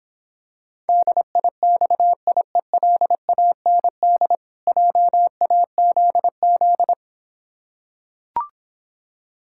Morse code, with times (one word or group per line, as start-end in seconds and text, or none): 0.9-4.3 DIXIELAND
4.7-6.9 JAZZ